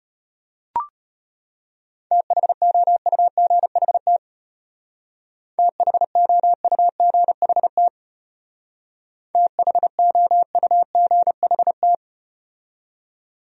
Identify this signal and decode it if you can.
Morse code, heard as a keyed tone, about 35 wpm, 700 Hz. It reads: THOUGHT THOUGHT THOUGHT